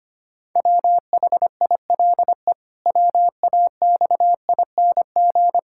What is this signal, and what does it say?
Morse code, heard as a keyed tone, about 25 wpm, 700 Hz. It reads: WHILE WAXING
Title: Pangram Challenge v2 - 25wpm